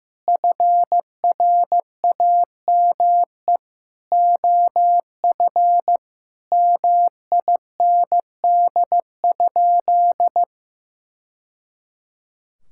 Morse code, heard as a keyed tone, about 15 wpm, 700 Hz. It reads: FRAME OF MIND?
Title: Sentences from Top 1000 Words - Review Part 6 - 15wpm